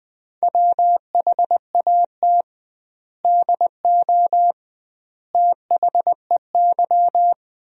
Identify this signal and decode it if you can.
Morse code, heard as a keyed tone, about 20 wpm, 700 Hz. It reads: WHAT DO THEY
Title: 2x Word-Spacing - Sentences from Top 200 Words - 20wpm